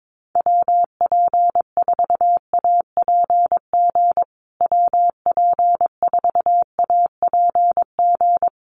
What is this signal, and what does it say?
Morse code, heard as a keyed tone, about 22 wpm, 700 Hz. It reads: WP4APG WP4APG
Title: USA Call Signs Repeated 3x 22wpm